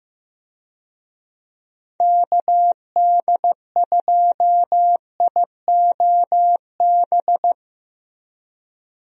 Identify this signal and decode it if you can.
Morse code, heard as a keyed tone, about 15 wpm, 700 Hz. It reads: KD2IOB
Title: USA Call Signs - 6 Characters - Repeated 3x - 15wpm